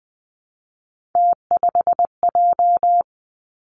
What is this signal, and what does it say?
Morse code, heard as a keyed tone, about 20 wpm, 700 Hz. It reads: T5J
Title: Three Challenging Letter-Number Combinations 20wpm